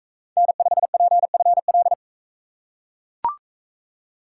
Morse code, heard as a keyed tone, about 39 wpm, 700 Hz. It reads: N5PFL